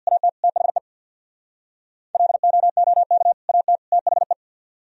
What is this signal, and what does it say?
Morse code, heard as a keyed tone, about 63 wpm, 700 Hz. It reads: AT THE LOOK AT THE